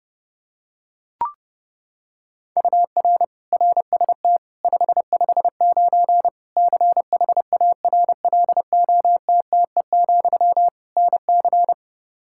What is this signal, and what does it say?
Morse code, heard as a keyed tone, about 30 wpm, 700 Hz. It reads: UR RST 559 CHARLOTTE, NC
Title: Practice QSO - Exchange 1 - Beginning - 30wpm